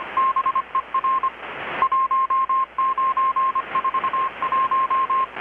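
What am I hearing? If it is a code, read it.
Morse code, heard as a keyed tone, about 25 wpm, 1060 Hz. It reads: BER 1941
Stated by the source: Morse from KPH's Night of Nights XX maritime radiotelegraphy event, received in Chiapas, Mexico about 3,800 km away on a Kenwood R-600 with a grounded 10 m long-wire antenna.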